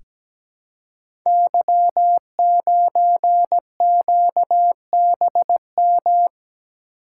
Morse code, heard as a keyed tone, about 17 wpm, 700 Hz. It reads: Y9QBM
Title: Code Groups - 5 Letter:Numbers - 17wpm